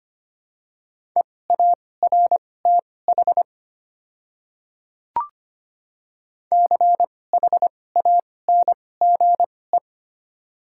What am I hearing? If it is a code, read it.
Morse code, heard as a keyed tone, about 25 wpm, spaced as for 18 wpm, 690 Hz. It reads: EARTH CHANGE
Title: 2x Character Spacing - Top 200 Words - 25wpm